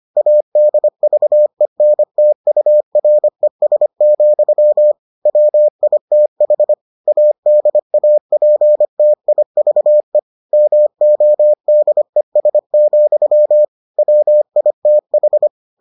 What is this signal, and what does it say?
Morse code, heard as a keyed tone, about 25 wpm, 600 Hz. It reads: ADVENTURES, WITH ADAPTIVE MODES, WITH